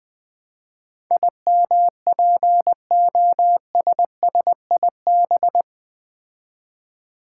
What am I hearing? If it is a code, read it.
Morse code, heard as a keyed tone, about 20 wpm, 700 Hz. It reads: IMPOSSIB
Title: Build A Word - 4 Letters At A Time - 10-Letter Words - 20wpm